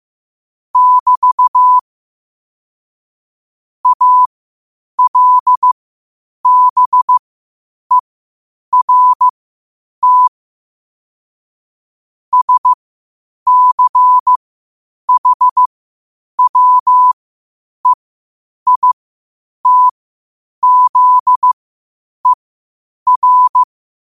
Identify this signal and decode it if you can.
Morse code, heard as a keyed tone, about 15 wpm, spaced as for 8 wpm, 1000 Hz. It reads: <BT> ALBERT SCHWEITZER
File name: qotdfiles / 8wpm